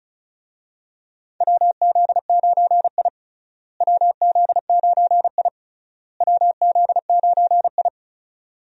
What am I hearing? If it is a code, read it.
Morse code, heard as a keyed tone, about 34 wpm, 700 Hz. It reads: WZ9I WZ9I WZ9I
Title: USA Call Signs - 4 Characters - Repeated 3x - 35wpm